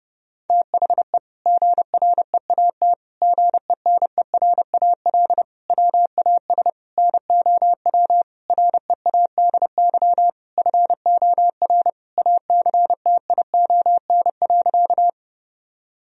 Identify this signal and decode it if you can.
Morse code, heard as a keyed tone, about 32 wpm, 700 Hz. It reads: THE GREAT GENERAL WAS NOW READY FOR ACTION.
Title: Sentences from Top 1000 Words - Review Part 4 - 30wpm